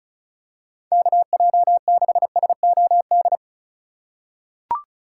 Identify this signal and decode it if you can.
Morse code, heard as a keyed tone, about 36 wpm, 700 Hz. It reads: KJ6SOD
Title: USA Call Signs - 6 Characters - 2x Character Spacing - 35wpm